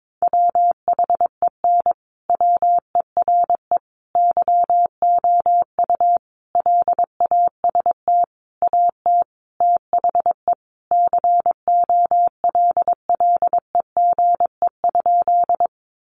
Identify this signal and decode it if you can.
Morse code, heard as a keyed tone, about 22 wpm, 700 Hz. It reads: WHEN WERE YOU LAST AT THE COLLEGE?
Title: Sentences from Top 1300 Words 22wpm